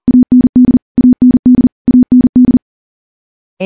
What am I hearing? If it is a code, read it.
Morse code, heard as a keyed tone, about 40 wpm, 250 Hz. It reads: AND AND AND
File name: morse / morse-training-40WPM